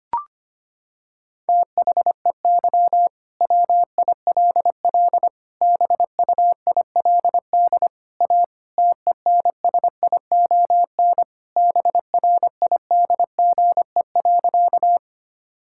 Morse code, heard as a keyed tone, about 25 wpm, 700 Hz. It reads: THEY WILL BUILD A TENSION BRIDGE.
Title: Sentences from Top 1600 Words 25wpm